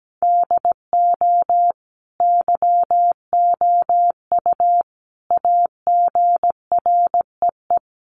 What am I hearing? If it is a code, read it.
Morse code, heard as a keyed tone, about 17 wpm, 700 Hz. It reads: DO YOU AGREE
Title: Sets of 5 Words Encore 17wpm